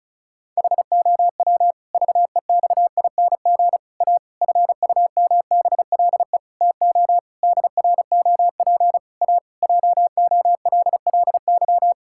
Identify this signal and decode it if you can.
Morse code, heard as a keyed tone, about 36 wpm, 700 Hz. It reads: HOW VEXING A FUMBLE TO DROP A JOLLY
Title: Pangram Challenge v2 - 35wpm